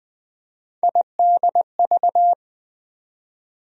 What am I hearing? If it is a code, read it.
Morse code, heard as a keyed tone, about 20 wpm, 700 Hz. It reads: IDV
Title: USA Call Sign Postfixes - 3 Letters 20wpm